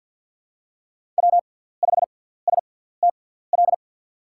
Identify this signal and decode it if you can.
Morse code, heard as a keyed tone, about 53 wpm, spaced as for 17 wpm, 700 Hz. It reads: W5STL
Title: USA Call Signs - 5 Characters - 6x Character Spacing - 50wpm